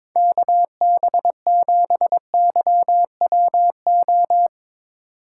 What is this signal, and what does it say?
Morse code, heard as a keyed tone, about 22 wpm, 700 Hz. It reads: KB7YWO
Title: USA Call Signs - 6 Characters - 6x Character Spacing - 22wpm